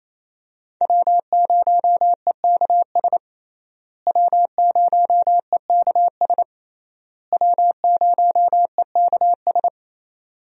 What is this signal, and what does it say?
Morse code, heard as a keyed tone, about 28 wpm, 700 Hz. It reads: W0EKS W0EKS W0EKS
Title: USA Call Signs - 5 Characters - Repeated 3x - 28wpm